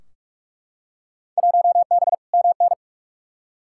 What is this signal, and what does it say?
Morse code, heard as a keyed tone, about 43 wpm, 700 Hz. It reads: 1B MN
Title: Field Day Exchanges Encore - 45wpm